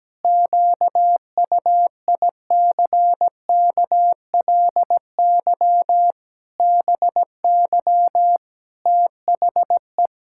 Morse code, heard as a keyed tone, about 17 wpm, 700 Hz. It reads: QUICKLY BY THE
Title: Pangram Challenge v2 - 17wpm